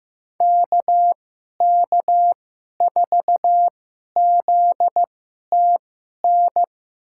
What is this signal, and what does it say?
Morse code, heard as a keyed tone, about 15 wpm, spaced as for 11 wpm, 700 Hz. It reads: KK4ZTN
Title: USA Call Signs - 6 Characters - 2x Character Spacing - 15wpm